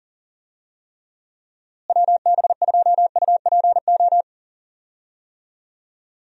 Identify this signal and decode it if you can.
Morse code, heard as a keyed tone, about 40 wpm, 700 Hz. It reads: WB2UPO
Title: USA Call Signs - 6 Characters - Repeated 3x - 40wpm